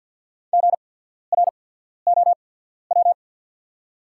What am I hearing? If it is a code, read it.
Morse code, heard as a keyed tone, about 48 wpm, spaced as for 14 wpm, 690 Hz. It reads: GROW
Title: 8x Character Spacing - Top 300 Words - 50wpm